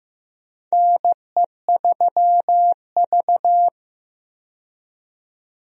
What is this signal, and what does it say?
Morse code, heard as a keyed tone, about 15 wpm, 700 Hz. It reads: NE3V
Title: USA Call Signs - 4 Characters - 15wpm